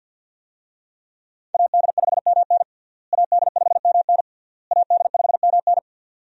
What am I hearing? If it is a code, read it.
Morse code, heard as a keyed tone, about 49 wpm, 700 Hz. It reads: AD5MN AD5MN AD5MN
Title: USA Call Signs - 5 Characters - Repeated 3x - 50wpm